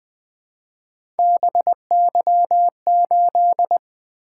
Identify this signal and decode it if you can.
Morse code, heard as a keyed tone, about 20 wpm, 700 Hz. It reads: BY8
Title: Three Letter-Number Combinations - 20wpm